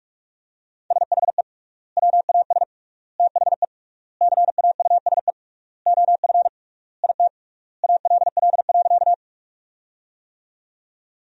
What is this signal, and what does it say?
Morse code, heard as a keyed tone, about 46 wpm, 700 Hz. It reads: SHE WAS THE CAUSE OF IT ALL.
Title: 3x Word-Spacing - Sentences from Top 200 Words - 45wpm